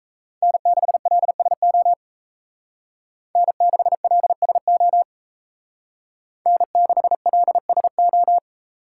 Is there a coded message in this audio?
Yes: Morse code from 0.4 to 8.4 s, heard as a keyed tone, about 36 wpm, 700 Hz. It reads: N6LSO N6LSO N6LSO